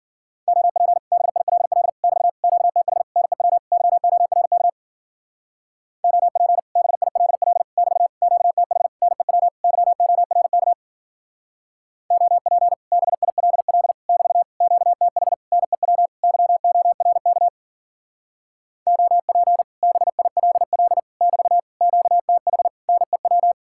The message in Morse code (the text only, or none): OP BILL <BT> QTH NEW YORK OP BILL <BT> QTH NEW YORK OP BILL <BT> QTH NEW YORK OP BILL <BT> QTH NEW